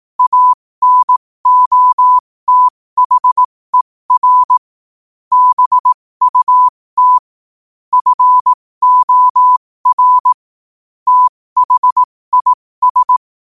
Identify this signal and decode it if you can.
Morse code, heard as a keyed tone, about 18 wpm, spaced as for 15 wpm, 1000 Hz. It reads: ANOTHER BUT FOR THIS
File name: qotdfiles / 15wpm